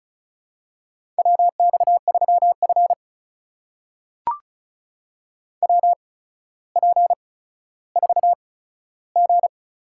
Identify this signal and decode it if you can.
Morse code, heard as a keyed tone, about 36 wpm, spaced as for 15 wpm, 700 Hz. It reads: WX3F WP4G
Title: USA Call Signs - 4 Characters - 8x Character Spacing - 35wpm